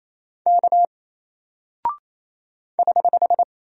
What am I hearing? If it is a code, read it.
Morse code, heard as a keyed tone, about 28 wpm, 700 Hz. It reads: K <HH>
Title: Single Character (Letter:Number:Punctuation:Prosign) 28wpm